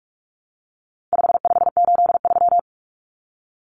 Morse code, heard as a keyed tone, about 44 wpm, 700 Hz. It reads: 5583
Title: Common Cut Numbers - 4-Digit 45wpm